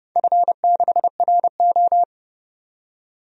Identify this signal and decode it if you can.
Morse code, heard as a keyed tone, about 30 wpm, 700 Hz. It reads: F6RO
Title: DXCC-100 Call Signs - 4 Characters - 30wpm